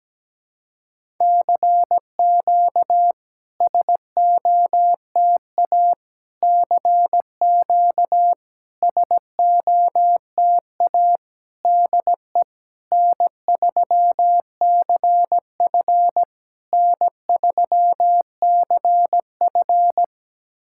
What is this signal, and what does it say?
Morse code, heard as a keyed tone, about 17 wpm, 700 Hz. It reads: CQ SOTA CQ SOTA DE N3CF N3CF
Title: SOTA QSO Practice 17wpm